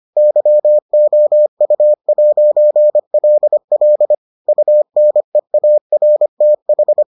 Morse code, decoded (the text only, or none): YOU'LL UNEARTH